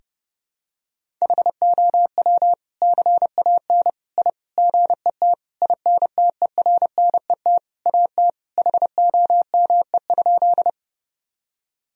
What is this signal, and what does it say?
Morse code, heard as a keyed tone, about 30 wpm, 700 Hz. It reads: HOW CAN I GET INTERNET AT HOME?
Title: Sentences from Top 1000 Words - Review Part 4 - 30wpm